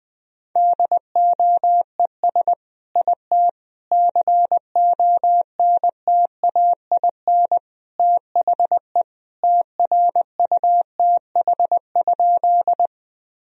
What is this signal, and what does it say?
Morse code, heard as a keyed tone, about 20 wpm, 700 Hz. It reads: DOES IT CONTAIN THE TRUTH?
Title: Sentences from Top 1000 Words - Review Part 2 - 20wpm